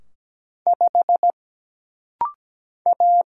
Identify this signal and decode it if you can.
Morse code, heard as a keyed tone, about 17 wpm, 700 Hz. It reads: 5 A
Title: Single Letter-Number - Rapid-Fire 17wpm